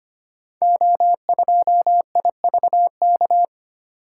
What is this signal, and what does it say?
Morse code, heard as a keyed tone, about 25 wpm, 700 Hz. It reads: O2IVK
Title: Code Groups - 5 Letter:Numbers - 25wpm